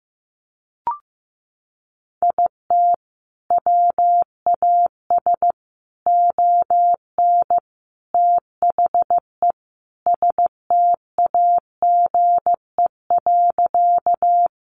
Morse code, heard as a keyed tone, about 15 wpm, 700 Hz. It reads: IT WAS ON THE STAGE.
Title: Sentences from Top 1200 Words 15wpm